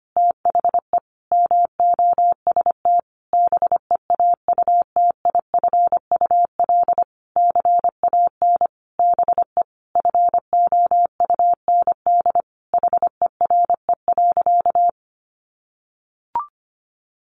Morse code, heard as a keyed tone, about 25 wpm, 700 Hz. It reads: THE MOST BEAUTIFUL CAN BE FOUND HERE.
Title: Sentences from Top 1500 Words 25wpm